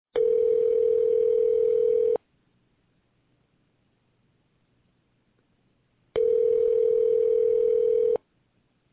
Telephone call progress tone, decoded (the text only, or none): ringback tone